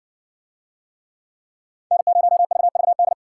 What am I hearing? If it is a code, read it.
Morse code, heard as a keyed tone, about 60 wpm, 700 Hz. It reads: N0VVD